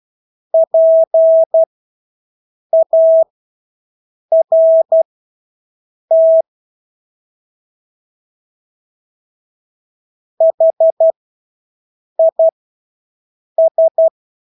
Morse code, heard as a keyed tone, about 12 wpm, spaced as for 6 wpm, 650 Hz. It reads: PART HIS
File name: Engelske ord 12-6 wpm